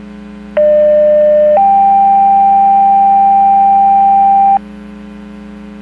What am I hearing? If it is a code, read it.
Two-tone page, heard as a one-second tone, then a three-second tone, about 600 then 790 Hz. It is heard from 0.6 to 4.6 s.
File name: Portage 154.385 Paging